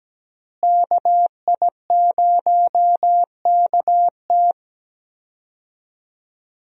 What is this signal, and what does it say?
Morse code, heard as a keyed tone, about 17 wpm, 700 Hz. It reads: KI0KT
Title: USA Call Signs - 5 Characters - Repeated 3x - 17wpm